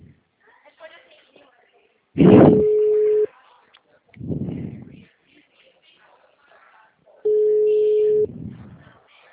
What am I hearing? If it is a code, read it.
Ringback tone (Europe, 425 Hz), with speech.
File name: out- -2014- - - .64268